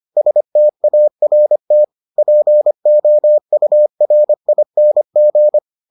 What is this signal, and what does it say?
Morse code, heard as a keyed tone, about 25 wpm, 600 Hz. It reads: START POURING